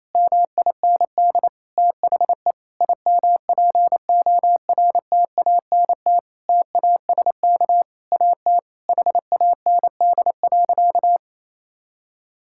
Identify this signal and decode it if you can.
Morse code, heard as a keyed tone, about 28 wpm, 700 Hz. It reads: MIND THE IMPORTANT TASK AT HAND.
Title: Sentences from Top 1600 Words 28wpm